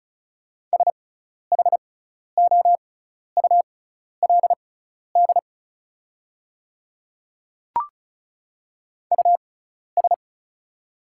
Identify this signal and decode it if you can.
Morse code, heard as a keyed tone, about 34 wpm, spaced as for 12 wpm, 690 Hz. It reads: SHOULD US
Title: 6x Character Spacing - Top 300 Words - 35wpm